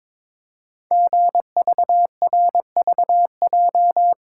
Morse code, heard as a keyed tone, about 22 wpm, 700 Hz. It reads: GVRVJ
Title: Code Groups - 5 Letter:Numbers - 22wpm